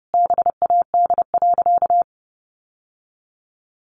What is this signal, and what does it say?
Morse code, heard as a keyed tone, about 30 wpm, 700 Hz. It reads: BAD.
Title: Sentences from Top 700 Words 30wpm